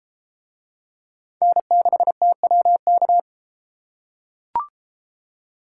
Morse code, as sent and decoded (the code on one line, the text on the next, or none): -. -.... - .-- -.-
N6TWK